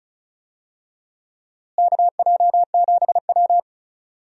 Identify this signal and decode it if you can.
Morse code, heard as a keyed tone, about 36 wpm, 700 Hz. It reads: KJ7W
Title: USA Call Signs - 4 Characters - Repeated 2x - 35wpm